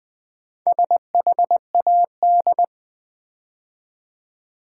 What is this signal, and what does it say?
Morse code, heard as a keyed tone, about 20 wpm, 700 Hz. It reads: SHAD
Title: Build A Word - 2 Letters At A Time - 6-Letter Words - 20wpm